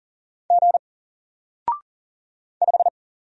Morse code, heard as a keyed tone, about 40 wpm, 700 Hz. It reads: G 5
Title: Single Letter-Number 40wpm